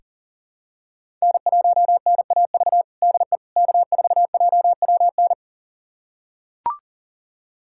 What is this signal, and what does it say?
Morse code, heard as a keyed tone, about 40 wpm, 700 Hz. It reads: N1NAV DE K4JWN
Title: USA Callsign DE Callsign - 5 Characters - 40wpm